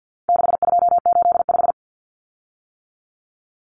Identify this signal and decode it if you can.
Morse code, heard as a keyed tone, about 50 wpm, 700 Hz. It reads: 6285
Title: Common Cut Numbers - 4-Digit 50wpm